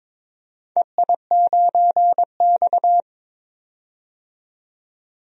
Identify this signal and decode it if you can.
Morse code, heard as a keyed tone, about 22 wpm, 700 Hz. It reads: EI9X
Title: DXCC-100 Call Signs - 4 Characters - 22wpm